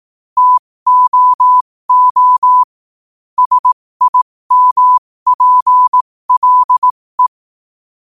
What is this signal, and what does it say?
Morse code, heard as a keyed tone, about 18 wpm, spaced as for 15 wpm, 1000 Hz. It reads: TOO SIMPLE
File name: qotdfiles / 15wpm